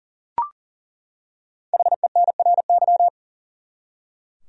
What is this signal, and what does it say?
Morse code, heard as a keyed tone, about 39 wpm, 700 Hz. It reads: HENRY
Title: USA Common Names - 40wpm